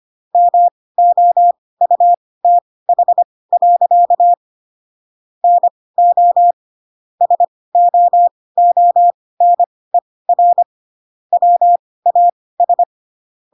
Morse code, heard as a keyed tone, about 25 wpm, spaced as for 18 wpm, 700 Hz. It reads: MOUTH. NO SOONER WAS
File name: Pinocchio ch3 12 25 ik1hge